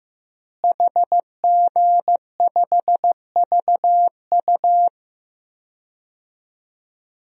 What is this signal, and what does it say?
Morse code, heard as a keyed tone, about 15 wpm, 700 Hz. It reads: HG5VU
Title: DXCC-100 Call Signs - 5 Characters - Repeated 3x - 15wpm